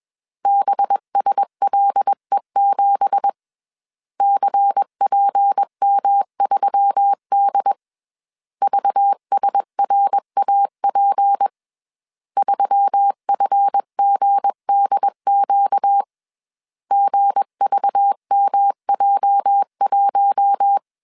Morse code, heard as a keyed tone, about 21 wpm, 800 Hz. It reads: BSLE7 CPM3D VSRAP 3FGDQ GVMJ1
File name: CW / DET / 1 9 1500C S C 20wpm 2310